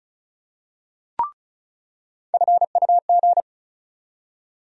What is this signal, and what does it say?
Morse code, heard as a keyed tone, about 36 wpm, 700 Hz. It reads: FUG
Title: USA Call Sign Postfixes - 3 Letters 35wpm